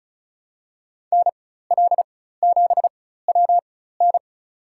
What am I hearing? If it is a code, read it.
Morse code, heard as a keyed tone, about 34 wpm, spaced as for 16 wpm, 700 Hz. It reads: NL7WN
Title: USA Call Signs - 5 Characters - 4x Character Spacing - 35wpm